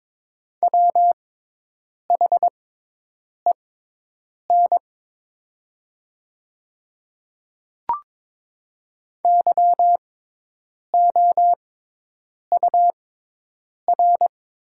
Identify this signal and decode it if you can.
Morse code, heard as a keyed tone, about 22 wpm, spaced as for 8 wpm, 700 Hz. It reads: WHEN YOUR
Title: 6x Character Spacing - Top 100 Words - 22wpm